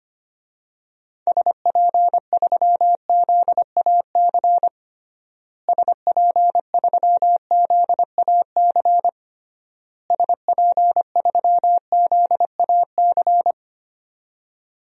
Morse code, heard as a keyed tone, about 25 wpm, 700 Hz. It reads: SP3ZAC SP3ZAC SP3ZAC
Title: DXCC-100 Call Signs - 6 Characters - Repeated 3x - 25wpm